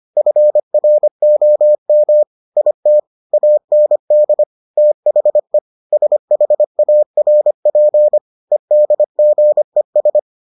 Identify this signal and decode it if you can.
Morse code, heard as a keyed tone, about 25 wpm, 600 Hz. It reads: FROM IT AND THE SHARP EDGES